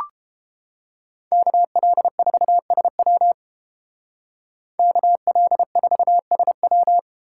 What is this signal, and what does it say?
Morse code, heard as a keyed tone, about 30 wpm, 700 Hz. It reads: KL4SW KL4SW